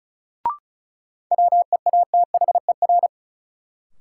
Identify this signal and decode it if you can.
Morse code, heard as a keyed tone, about 34 wpm, 700 Hz. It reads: WEATHER
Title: Sentences from Top 800 Words - New Word Drill - 35wpm